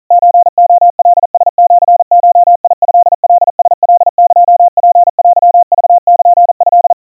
Morse code (Japanese Wabun code), heard as a keyed tone, about 40 wpm, 700 Hz. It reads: ソレカラシゴトカラカエツテクルト
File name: kawa / Morse sample / kamyu 150 cpm 700 Hz Crispy